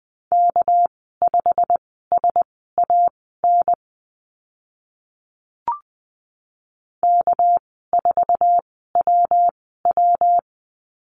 Morse code, heard as a keyed tone, about 20 wpm, spaced as for 15 wpm, 700 Hz. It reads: K5SAN K4WW
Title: 200 Lesson 32g - Letter k - Call Signs 20wpm